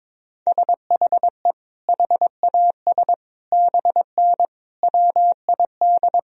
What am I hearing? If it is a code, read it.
Morse code, heard as a keyed tone, about 22 wpm, 700 Hz. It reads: SHE HAS BN WID
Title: Sentences from Top 200 Words and Abbreviations 22wpm